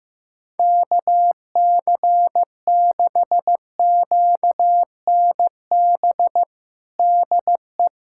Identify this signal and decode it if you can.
Morse code, heard as a keyed tone, about 15 wpm, 700 Hz. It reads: KC6QNB DE
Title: USA Callsign DE Callsign - 6 Characters - 15wpm